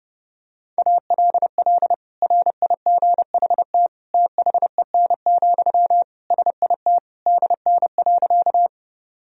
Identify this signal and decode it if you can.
Morse code, heard as a keyed tone, about 32 wpm, 700 Hz. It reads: ALL RIGHT THEN, SIT DN.
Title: Sentences from Top 500 Words and Abbreviations 30wpm